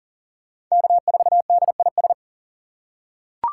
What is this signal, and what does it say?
Morse code, heard as a keyed tone, about 40 wpm, 700 Hz. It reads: K4DIS
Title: USA Call Signs - 5 Characters - 6x Character Spacing - 40wpm